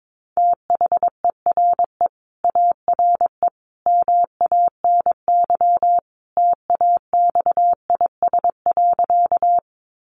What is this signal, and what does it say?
Morse code, heard as a keyed tone, about 22 wpm, 700 Hz. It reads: THERE ARE MANY TAXIS.
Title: Easy Phrases 22wpm